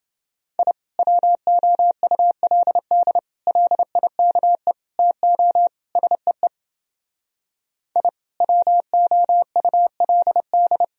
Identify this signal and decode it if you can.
Morse code, heard as a keyed tone, about 28 wpm, 700 Hz. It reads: I WOULD LIKE TO SEE I WOULD